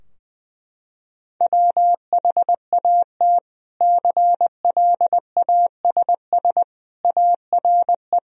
Morse code, heard as a keyed tone, about 20 wpm, 700 Hz. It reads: WHAT CLASS ARE
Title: Sentences from Top 1000 Words - Review Part 2 - 20wpm